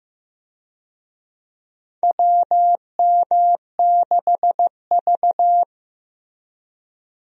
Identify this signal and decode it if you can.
Morse code, heard as a keyed tone, about 15 wpm, 700 Hz. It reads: WM6V
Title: USA Call Signs - 4 Characters - Repeated 2x - 15wpm